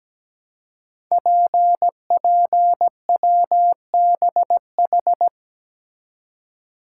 Morse code, heard as a keyed tone, about 17 wpm, 700 Hz. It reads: PPWBH
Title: Code Groups - 5 Letters - 17wpm